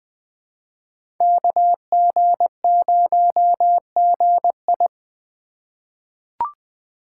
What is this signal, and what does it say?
Morse code, heard as a keyed tone, about 20 wpm, 700 Hz. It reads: KG0GI
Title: USA Call Signs - 5 Characters - 3x Character Spacing - 20wpm